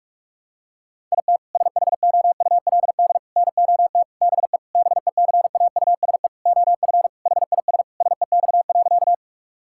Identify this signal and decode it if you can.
Morse code, heard as a keyed tone, about 46 wpm, 700 Hz. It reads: IT SHOULD NOT BE BECAUSE OF HIS SEX.
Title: Sentences from Top 1600 Words 45wpm